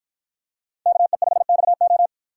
Morse code, heard as a keyed tone, about 53 wpm, 700 Hz. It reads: KE5XO